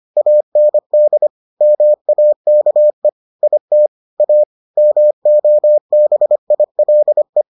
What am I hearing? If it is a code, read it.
Morse code, heard as a keyed tone, about 25 wpm, 600 Hz. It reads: AND MAKE IT A MOBILE